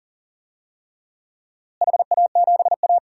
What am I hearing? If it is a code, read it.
Morse code, heard as a keyed tone, about 40 wpm, 700 Hz. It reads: HA7A
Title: DXCC-100 Call Signs - 4 Characters - Repeated 3x - 40wpm